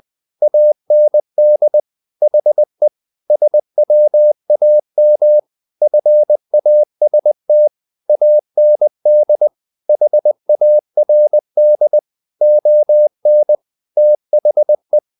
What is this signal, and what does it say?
Morse code, heard as a keyed tone, about 20 wpm, 600 Hz. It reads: AND HE SWAM FAST AND HARD ON THE
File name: the old man and the sea part 2 20wpm